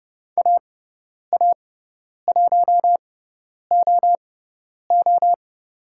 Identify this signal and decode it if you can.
Morse code, heard as a keyed tone, about 30 wpm, spaced as for 10 wpm, 700 Hz. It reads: AA1OO
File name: Lesson 16g - Letter c - Call Signs 30wpm